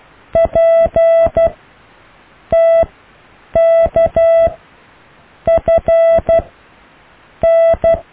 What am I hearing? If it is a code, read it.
Morse code, heard as a keyed tone, about 12 wpm, spaced as for 6 wpm, 660 Hz. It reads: PTKFN
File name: cw / EKTPV+BFHNW